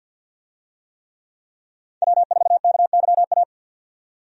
Morse code, heard as a keyed tone, about 53 wpm, 700 Hz. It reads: W4KYA
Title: USA Call Signs - 5 Characters - 50wpm